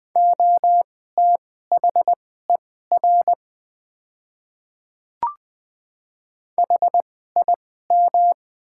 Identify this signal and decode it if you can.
Morse code, heard as a keyed tone, about 20 wpm, spaced as for 14 wpm, 700 Hz. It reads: OTHER HIM
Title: 2x Character Spacing - Top 100 Words - 20wpm